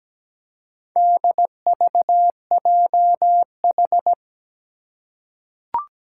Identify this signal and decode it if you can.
Morse code, heard as a keyed tone, about 17 wpm, 700 Hz. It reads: DVJH
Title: Four Challenging Letter-Number Combinations 17wpm